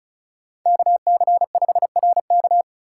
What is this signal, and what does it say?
Morse code, heard as a keyed tone, about 34 wpm, 700 Hz. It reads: KC5RK
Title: USA Call Signs - 5 Characters - 4x Character Spacing - 35wpm